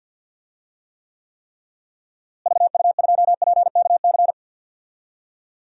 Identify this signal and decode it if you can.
Morse code, heard as a keyed tone, about 48 wpm, 700 Hz. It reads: VU2PKC
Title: DXCC-100 Call Signs - 6 Characters - Repeated 3x - 50wpm